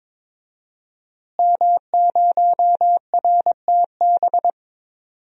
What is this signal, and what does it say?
Morse code, heard as a keyed tone, about 22 wpm, 700 Hz. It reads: M0RTB
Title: DXCC-100 Call Signs - 5 Characters - Repeated 2x - 22wpm